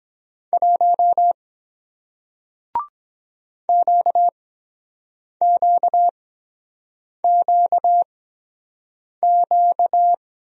Morse code, heard as a keyed tone, about 23 wpm, spaced as for 7 wpm, 700 Hz. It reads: 1 QQQQ